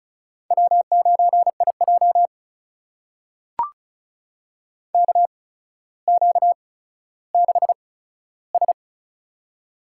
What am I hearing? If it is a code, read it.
Morse code, heard as a keyed tone, about 33 wpm, spaced as for 15 wpm, 700 Hz. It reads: W9IJ KQ6S
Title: USA Call Signs - 4 Characters - 8x Character Spacing - 35wpm